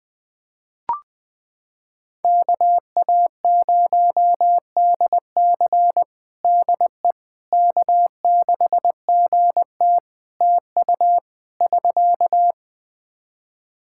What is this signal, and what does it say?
Morse code, heard as a keyed tone, about 20 wpm, 700 Hz. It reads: KA0DC DE K6GT TU <SK>
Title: Practice QSO - Exchange 4 - Ending - 20wpm